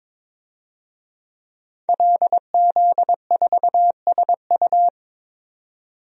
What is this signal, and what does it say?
Morse code, heard as a keyed tone, about 22 wpm, 700 Hz. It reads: LZ4SU
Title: DXCC-100 Call Signs - 5 Characters - 22wpm